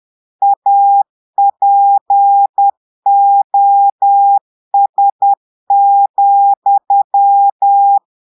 Morse code, heard as a keyed tone, about 10 wpm, 800 Hz. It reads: APOS,
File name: portugais texte en clair 2 vitesse 10 mots minutes